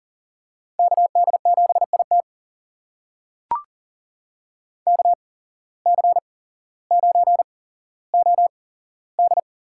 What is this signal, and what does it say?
Morse code, heard as a keyed tone, about 40 wpm, spaced as for 17 wpm, 700 Hz. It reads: KD7IT KC9OD
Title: USA Call Signs - 5 Characters - 8x Character Spacing - 40wpm